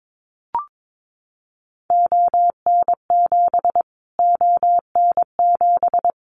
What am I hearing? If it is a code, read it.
Morse code, heard as a keyed tone, about 22 wpm, 700 Hz. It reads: ON7 ON7
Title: Callsign Prefixes for Top 25 DXCC - Repeated 3x - 22wpm